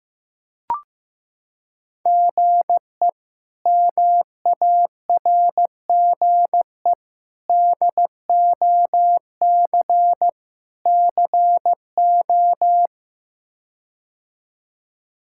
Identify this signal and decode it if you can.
Morse code, heard as a keyed tone, about 16 wpm, 700 Hz. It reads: GE MARGE DOC CO
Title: K1USN SST QSO Practice - 15wpm